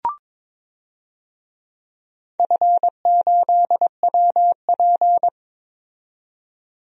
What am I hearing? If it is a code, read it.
Morse code, heard as a keyed tone, about 22 wpm, 700 Hz. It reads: F8WP
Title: DXCC-100 Call Signs - 4 Characters - 22wpm